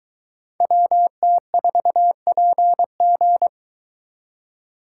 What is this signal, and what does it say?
Morse code, heard as a keyed tone, about 23 wpm, 700 Hz. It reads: WT4PG